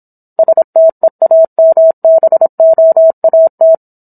Morse code, heard as a keyed tone, about 27 wpm, 650 Hz. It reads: STEAMBOAT
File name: Book One / 17-Chapter 17 - 26 WPM